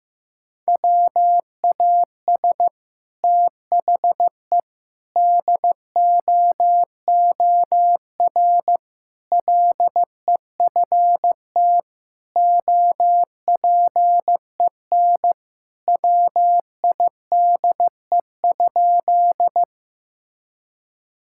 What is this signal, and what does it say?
Morse code, heard as a keyed tone, about 15 wpm, 700 Hz. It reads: WAS THE DOOR LEFT OPEN WIDE?
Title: Sentences from Top 1000 Words - Review Part 6 - 15wpm